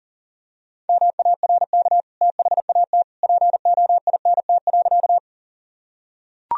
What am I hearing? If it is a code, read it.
Morse code, heard as a keyed tone, about 40 wpm, 700 Hz. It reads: MARK THAT POINT.
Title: Sentences from Top 1000 Words - Review Part 1 - 40wpm